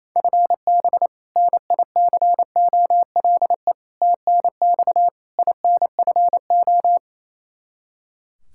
Morse code, heard as a keyed tone, about 28 wpm, 700 Hz. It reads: FB NICOLE TNX INFO
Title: Practice QSO - Exchange 3 - More Info - 28wpm